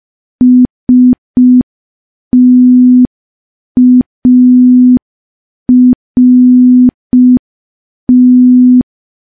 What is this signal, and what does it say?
Morse code, heard as a keyed tone, about 5 wpm, 250 Hz. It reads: START
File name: morse / morse-training-05WPM